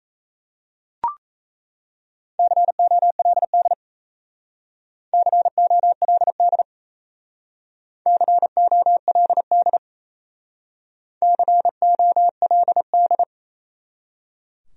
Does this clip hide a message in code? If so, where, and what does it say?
Yes: Morse code from 2.4 to 13.2 s, heard as a keyed tone, about 37 wpm, 700 Hz. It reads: COLD COLD COLD COLD